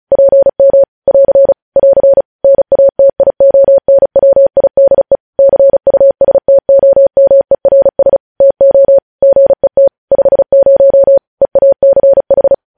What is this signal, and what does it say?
Morse code, heard as a keyed tone, about 36 wpm, 550 Hz. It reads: PM <AR> <AR> NATIONWIDE CUSTOMERS TO GET 50 EACH
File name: podcast / News headlines-1x-35x35